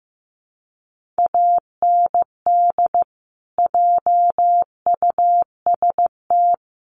Morse code, heard as a keyed tone, about 15 wpm, 700 Hz. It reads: AND JUST
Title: Sentences from Top 1500 Words 15wpm